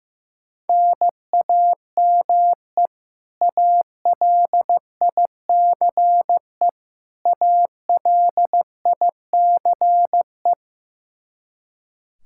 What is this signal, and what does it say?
Morse code, heard as a keyed tone, about 15 wpm, 700 Hz. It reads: NAME ALICE ALICE
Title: Practice QSO - Exchange 1 - Beginning - 15wpm